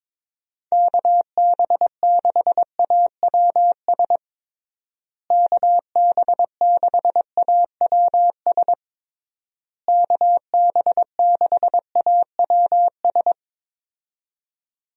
Morse code, heard as a keyed tone, about 22 wpm, 700 Hz. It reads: KB6AWS KB6AWS KB6AWS
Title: USA Call Signs - 6 Characters - Repeated 3x - 22wpm